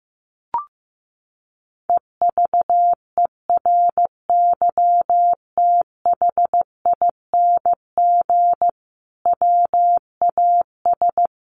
Morse code, heard as a keyed tone, about 15 wpm, 700 Hz. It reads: EVERYTHING WAS
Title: Sentences from Top 900 Words 15wpm